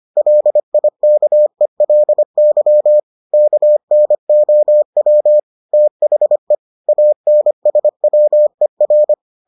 Morse code, heard as a keyed tone, about 25 wpm, 600 Hz. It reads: LIKELY KNOW THE ANSWER